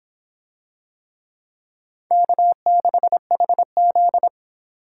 Morse code, heard as a keyed tone, about 26 wpm, 700 Hz. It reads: K6HZ